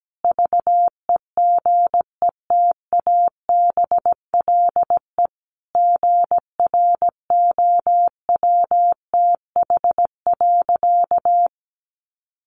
Morse code, heard as a keyed tone, about 17 wpm, 700 Hz. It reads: VEGETABLE GROWTH.
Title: Sentences from Top 1300 Words 17wpm